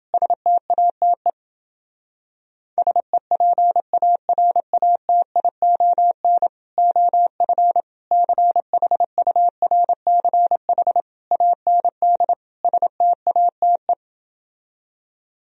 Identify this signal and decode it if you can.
Morse code, heard as a keyed tone, about 31 wpm, 700 Hz. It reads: STATE SEPARATION OF CHURCH AND STATE